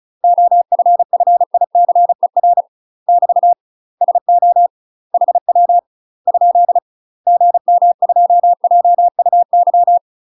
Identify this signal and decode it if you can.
Morse code, heard as a keyed tone, about 36 wpm, 700 Hz. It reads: OFFICER <BT> SO HW ? GM2JUY
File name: UK MASTER QSOs TRAINING/35WPM